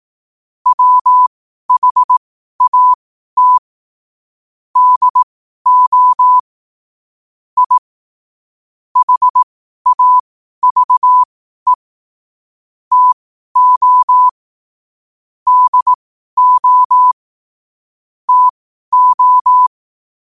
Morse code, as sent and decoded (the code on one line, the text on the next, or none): .-- .... .- - / -.. --- / .. / .... .- ...- . / - --- / -.. --- / - ---
WHAT DO I HAVE TO DO TO